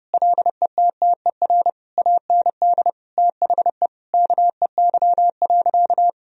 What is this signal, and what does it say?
Morse code, heard as a keyed tone, about 30 wpm, 700 Hz. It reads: LETTER AND THE KEY.
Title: Sentences from Top 1000 Words - Review Part 4 - 30wpm